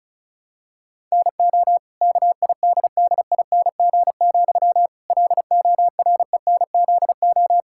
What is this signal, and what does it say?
Morse code, heard as a keyed tone, about 34 wpm, 700 Hz. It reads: NO KIDDING, LORENZO
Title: Pangram Challenge v2 - 35wpm